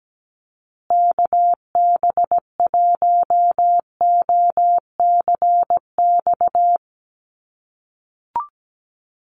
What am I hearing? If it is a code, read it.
Morse code, heard as a keyed tone, about 17 wpm, 700 Hz. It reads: KB1OCX
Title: USA Call Signs Repeated 3x 17wpm